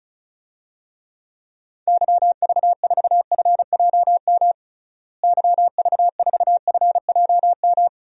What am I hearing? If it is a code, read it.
Morse code, heard as a keyed tone, about 34 wpm, 700 Hz. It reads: YV4FJM YV4FJM
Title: DXCC-100 Call Signs - 6 Characters - Repeated 2x - 35wpm